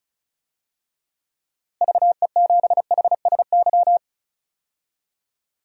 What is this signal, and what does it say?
Morse code, heard as a keyed tone, about 34 wpm, 700 Hz. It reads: VE7HSY
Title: DXCC-100 Call Signs - 6 Characters - Repeated 2x - 35wpm